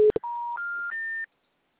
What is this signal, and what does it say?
Special information tone.
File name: out- -2044- - - .8959